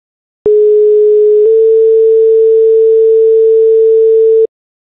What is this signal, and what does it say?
Two-tone page, heard as a one-second tone, then a three-second tone, about 420 then 450 Hz. It is heard from 0.5 to 4.5 s.